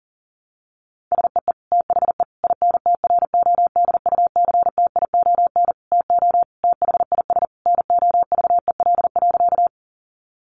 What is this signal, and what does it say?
Morse code, heard as a keyed tone, about 40 wpm, 700 Hz. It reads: SEE THE INTRODUCTION TO THIS NOVEL.
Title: Sentences from Top 1300 Words 40wpm